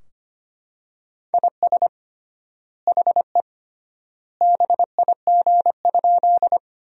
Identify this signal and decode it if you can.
Morse code, heard as a keyed tone, about 25 wpm, 700 Hz. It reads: IS HE BIG?
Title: 3x Word-Spacing - Sentences from Top 200 Words - 25wpm